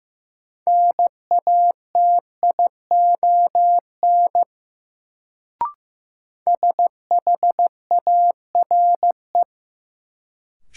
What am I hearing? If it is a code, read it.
Morse code, heard as a keyed tone, about 15 wpm, 700 Hz. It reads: NATION SHARE
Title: Sentences from Top 900 Words - New Word Drill - 15wpm